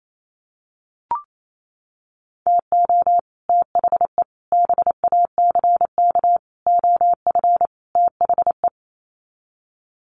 Morse code, heard as a keyed tone, about 28 wpm, 700 Hz. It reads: TO THE BACK OF THE
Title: Sets of 5 Words 28wpm